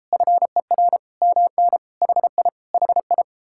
Morse code, heard as a keyed tone, about 33 wpm, 700 Hz. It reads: FER MN HI HI